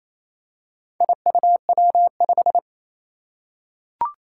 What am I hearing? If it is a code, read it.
Morse code, heard as a keyed tone, about 28 wpm, 700 Hz. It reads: IUW5
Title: Four Challenging Letter-Number Combinations 28wpm